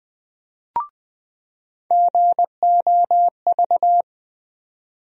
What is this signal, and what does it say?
Morse code, heard as a keyed tone, about 20 wpm, 700 Hz. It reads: GOV
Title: USA Call Sign Postfixes - 3 Letters 20wpm